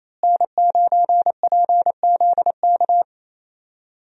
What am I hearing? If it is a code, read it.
Morse code, heard as a keyed tone, about 27 wpm, 700 Hz. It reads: N9PZK
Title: USA Call Signs - 5 Characters - Repeated 3x - 28wpm